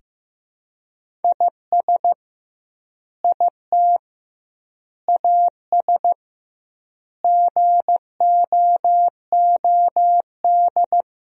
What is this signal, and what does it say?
Morse code, heard as a keyed tone, about 15 wpm, 700 Hz. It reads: IS IT AS GOOD
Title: 2x Word-Spacing - Sentences from Top 200 Words - 15wpm